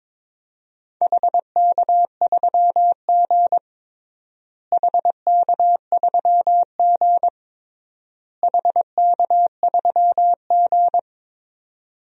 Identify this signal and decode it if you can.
Morse code, heard as a keyed tone, about 22 wpm, 700 Hz. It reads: HK3G HK3G HK3G
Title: DXCC-100 Call Signs - 4 Characters - Repeated 3x - 22wpm